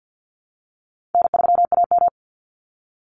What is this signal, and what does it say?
Morse code, heard as a keyed tone, about 50 wpm, 700 Hz. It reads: N3AM
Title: Call Signs - 50wpm